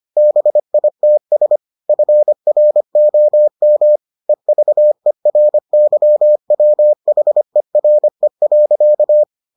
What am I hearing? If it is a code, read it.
Morse code, heard as a keyed tone, about 25 wpm, 600 Hz. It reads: BITS FROM EVERYWHERE.